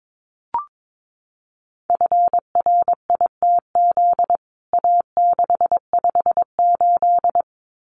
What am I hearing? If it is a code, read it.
Morse code, heard as a keyed tone, about 22 wpm, 700 Hz. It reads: FRITZ A658
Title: CWT Exchange 22wpm